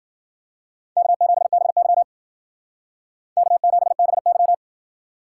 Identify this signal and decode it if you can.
Morse code, heard as a keyed tone, about 51 wpm, 700 Hz. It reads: K7BY K7BY